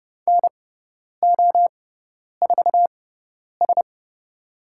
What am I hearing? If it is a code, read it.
Morse code, heard as a keyed tone, about 31 wpm, spaced as for 10 wpm, 700 Hz. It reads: NO4S
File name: Lesson 17g - Letter u - Call Signs 30wpm